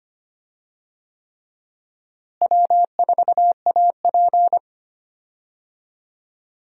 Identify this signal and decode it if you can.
Morse code, heard as a keyed tone, about 25 wpm, 700 Hz. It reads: W4AP
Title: USA Call Signs - 4 Characters - Repeated 3x - 25wpm